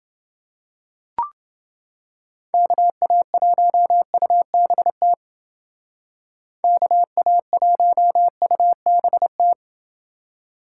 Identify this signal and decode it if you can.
Morse code, heard as a keyed tone, about 29 wpm, 700 Hz. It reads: KA1UBT KA1UBT